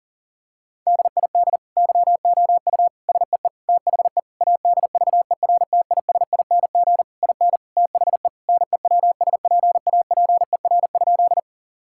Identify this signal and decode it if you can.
Morse code, heard as a keyed tone, about 37 wpm, 700 Hz. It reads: DID YOU SEE THE ADVERTISING IN THE NEWSPAPER?
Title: Sentences from Top 1000 Words - Review Part 6 - 40wpm